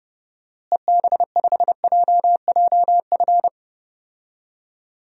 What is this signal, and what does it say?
Morse code, heard as a keyed tone, about 32 wpm, 700 Hz. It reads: EB5JJF
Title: DXCC-100 Call Signs - 6 Characters - Repeated 2x - 30wpm